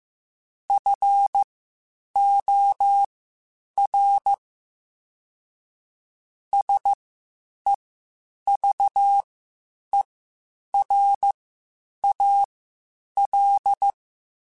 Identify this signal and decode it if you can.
Morse code, heard as a keyed tone, about 15 wpm, spaced as for 8 wpm, 800 Hz. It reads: FOR SEVERAL
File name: (spaced) / QSOs Spaced-mp3s / WA7QOX 2 @13 sp